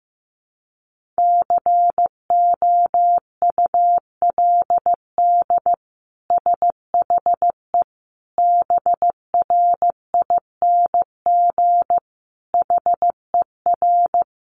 Morse code, heard as a keyed tone, about 15 wpm, 700 Hz. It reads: COULD SHE BRING HER
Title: Sentences from Top 1400 Words 15wpm